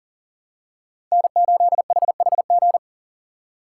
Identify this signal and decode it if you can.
Morse code, heard as a keyed tone, about 40 wpm, 700 Hz. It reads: N8HHG
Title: K1USN SST QSO Practice - 40wpm